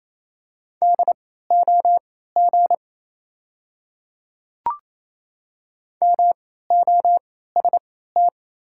Morse code, heard as a keyed tone, about 28 wpm, spaced as for 16 wpm, 690 Hz. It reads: DOG MOST
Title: 3x Character Spacing - Top 500 Words - 28wpm